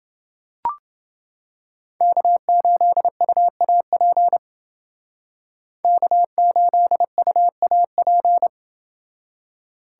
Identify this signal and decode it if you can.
Morse code, heard as a keyed tone, about 29 wpm, 700 Hz. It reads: K8UAP K8UAP